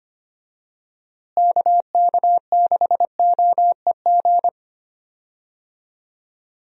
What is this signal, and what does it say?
Morse code, heard as a keyed tone, about 25 wpm, 700 Hz. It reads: KK6OEG
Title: USA Call Signs - 6 Characters - Repeated 3x - 25wpm